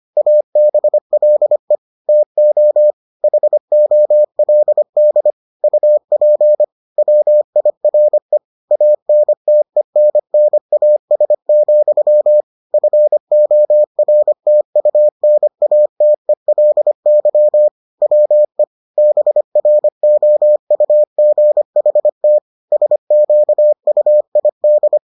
Morse code, heard as a keyed tone, about 25 wpm, 600 Hz. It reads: ABLE TO HOLD UP WIRE ANTENNAS, FORTUNATELY WE BROUGHT SQUID